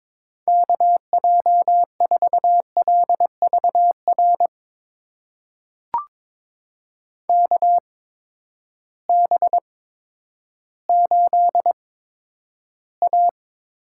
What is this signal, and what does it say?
Morse code, heard as a keyed tone, about 22 wpm, spaced as for 11 wpm, 700 Hz. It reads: KJ4LVR KB8A
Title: USA Call Signs - 6 Characters - 8x Character Spacing - 22wpm